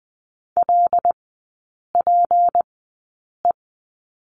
Morse code, heard as a keyed tone, about 20 wpm, spaced as for 8 wpm, 700 Hz. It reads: LPE
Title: 178 Lesson 29e - Slash - Three Characters 20wpm